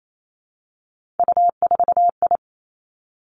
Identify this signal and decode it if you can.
Morse code, heard as a keyed tone, about 28 wpm, 700 Hz. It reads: U4I
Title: Three Challenging Letter-Number Combinations 28wpm